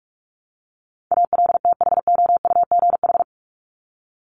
Morse code, heard as a keyed tone, about 44 wpm, 700 Hz. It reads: ALTHOUGH
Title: Top 501-750 Words 45wpm